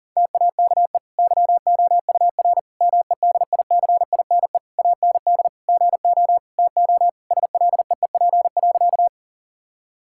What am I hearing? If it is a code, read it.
Morse code, heard as a keyed tone, about 40 wpm, 700 Hz. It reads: TAKE YOUR MEDICINE AND GO TO SLEEP.
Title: Sentences from Top 1000 Words - Review Part 6 - 40wpm